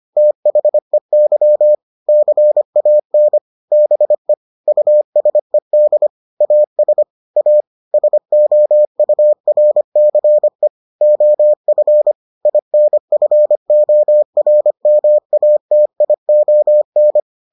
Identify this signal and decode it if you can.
Morse code, heard as a keyed tone, about 25 wpm, 600 Hz. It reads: THEY CAN BE USED AS A SOURCE OF INFORMATION